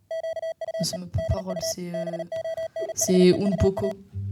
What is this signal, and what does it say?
Morse code, heard as a keyed tone, about 37 wpm, 660 Hz, with speech. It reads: QUE DU BONHEUA